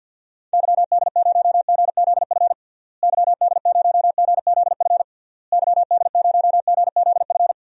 Morse code, heard as a keyed tone, about 48 wpm, 700 Hz. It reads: YD0GZF YD0GZF YD0GZF
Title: DXCC-100 Call Signs - 6 Characters - Repeated 3x - 50wpm